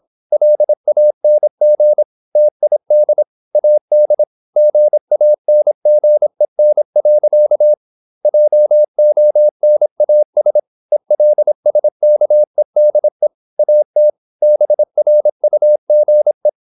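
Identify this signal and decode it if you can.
Morse code, heard as a keyed tone, about 26 wpm, 600 Hz. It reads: LANG TID AD GANGEN. JONAS ELSKEDE AT BRUGE
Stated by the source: High-speed (QRQ) Morse at 26 wpm.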